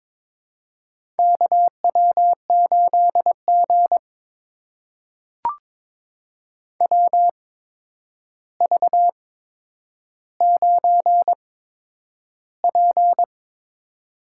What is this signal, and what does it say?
Morse code, heard as a keyed tone, about 22 wpm, spaced as for 9 wpm, 700 Hz. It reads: KW8G WV9P
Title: USA Call Signs - 4 Characters - 8x Character Spacing - 22wpm